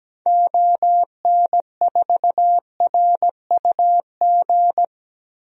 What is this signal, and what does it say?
Morse code, heard as a keyed tone, about 17 wpm, 700 Hz. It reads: ON4RUG
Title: DXCC-100 Call Signs - 6 Characters - 17wpm